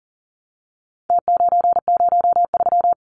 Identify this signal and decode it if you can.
Morse code, heard as a keyed tone, about 40 wpm, 700 Hz. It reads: T903
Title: Numbers - Contest Style - 4-Digit 40wpm